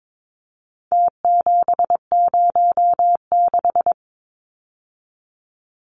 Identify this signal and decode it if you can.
Morse code, heard as a keyed tone, about 22 wpm, 700 Hz. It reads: T706
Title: Numbers - Contest Style - 4-Digit 22wpm